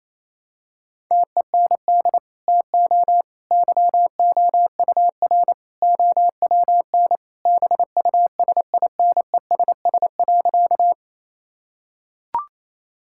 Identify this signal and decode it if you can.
Morse code, heard as a keyed tone, about 27 wpm, 700 Hz. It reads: TEND TO YOUR OWN BUSINESS.
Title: Sentences from Top 1000 Words - Review Part 5 - 28wpm